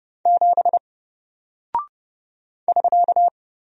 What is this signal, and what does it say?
Morse code, heard as a keyed tone, about 30 wpm, 700 Hz. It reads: 7 <SK>
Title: Single Character (Letter:Number:Punctuation:Prosign) Alt 30wpm